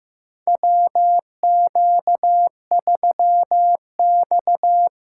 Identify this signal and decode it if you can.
Morse code, heard as a keyed tone, about 15 wpm, 700 Hz. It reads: WQ3X
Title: USA Call Signs - 4 Characters - 4x Character Spacing - 15wpm